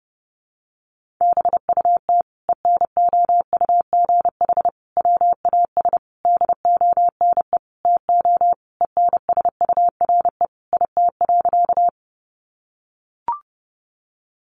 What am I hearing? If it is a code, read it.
Morse code, heard as a keyed tone, about 30 wpm, 700 Hz. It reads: BUT ENOUGH WAS DONE TO ENSURE IT.
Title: Sentences from Top 1100 Words 30wpm